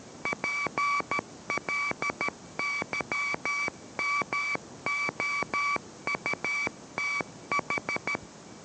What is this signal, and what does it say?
Morse code, heard as a keyed tone, about 13 wpm, 2300 Hz, with noise. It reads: PLYMOUTH